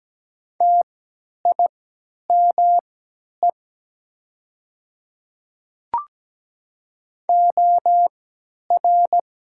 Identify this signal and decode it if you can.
Morse code, heard as a keyed tone, about 17 wpm, spaced as for 10 wpm, 700 Hz. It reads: TIME OR
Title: 3x Character Spacing - Top 100 Words - 17wpm